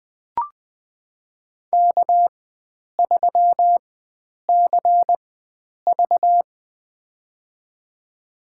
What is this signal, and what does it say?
Morse code, heard as a keyed tone, about 20 wpm, spaced as for 9 wpm, 700 Hz. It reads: K3CV
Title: USA Call Signs - 4 Characters - 4x Character Spacing - 20wpm